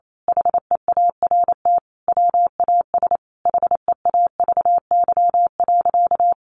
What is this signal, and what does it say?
Morse code, heard as a keyed tone, about 28 wpm, 700 Hz. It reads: HEART WAS HEAVY.
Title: Sentences from Top 1200 Words 28wpm